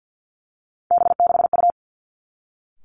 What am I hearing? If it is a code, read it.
Morse code, heard as a keyed tone, about 50 wpm, 700 Hz. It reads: B6U
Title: Three Challenging Letter-Number Combinations 50wpm